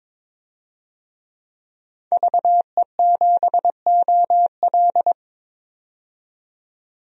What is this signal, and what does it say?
Morse code, heard as a keyed tone, about 22 wpm, 700 Hz. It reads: VE7OL
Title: DXCC-100 Call Signs - 5 Characters - 22wpm